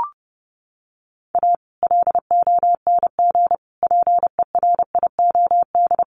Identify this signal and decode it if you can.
Morse code, heard as a keyed tone, about 30 wpm, 700 Hz. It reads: A LONG PERIOD
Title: Sets of 5 Words 30wpm